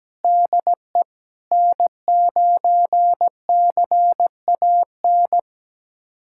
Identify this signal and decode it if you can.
Morse code, heard as a keyed tone, about 17 wpm, 700 Hz. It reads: DE N9CAN
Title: USA Callsign DE Callsign - 5 Characters - 17wpm